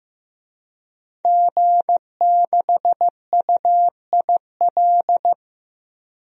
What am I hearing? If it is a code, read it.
Morse code, heard as a keyed tone, about 15 wpm, 700 Hz. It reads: G6UIL
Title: DXCC-100 Call Signs - 5 Characters - Repeated 3x - 15wpm